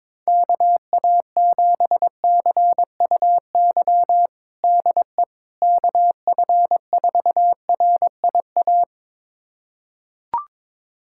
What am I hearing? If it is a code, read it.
Morse code, heard as a keyed tone, about 22 wpm, 700 Hz. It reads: KA7CUY DE KF4RIA
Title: USA Callsign DE Callsign 22wpm